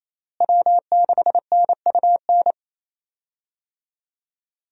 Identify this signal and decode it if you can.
Morse code, heard as a keyed tone, about 28 wpm, 700 Hz. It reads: W6NUN
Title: USA Call Signs - 5 Characters - Repeated 2x - 28wpm